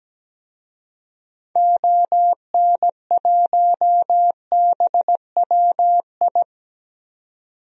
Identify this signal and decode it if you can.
Morse code, heard as a keyed tone, about 17 wpm, 700 Hz. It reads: ON1BWI
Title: DXCC-100 Call Signs - 6 Characters - 17wpm